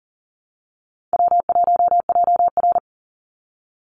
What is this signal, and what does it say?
Morse code, heard as a keyed tone, about 40 wpm, 700 Hz. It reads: W1JR
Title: Call Signs - 40wpm